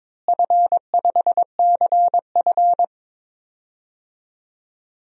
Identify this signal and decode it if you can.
Morse code, heard as a keyed tone, about 22 wpm, 700 Hz. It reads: F5CF
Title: DXCC-100 Call Signs - 4 Characters - Repeated 3x - 22wpm